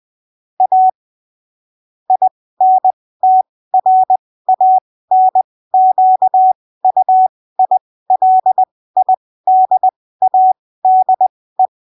Morse code, heard as a keyed tone, about 20 wpm, spaced as for 15 wpm, 750 Hz. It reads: A INTRANQUILIDADE
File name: arquivos / texto / texto 05 20ppm